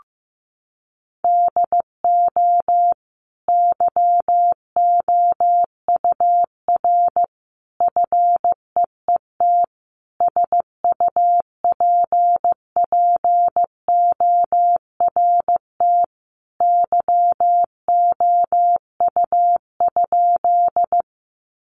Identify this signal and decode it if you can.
Morse code, heard as a keyed tone, about 15 wpm, 700 Hz. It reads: DO YOUR FEET SUPPORT YOU?
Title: Sentences from Top 800 Words 15wpm